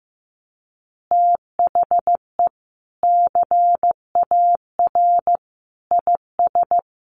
Morse code, heard as a keyed tone, about 15 wpm, 700 Hz. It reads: THE CAR IS
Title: Easy Phrases 15wpm